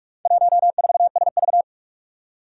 Morse code, heard as a keyed tone, about 46 wpm, 700 Hz. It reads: 14SV
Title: Four Challenging Letter-Number Combinations 45wpm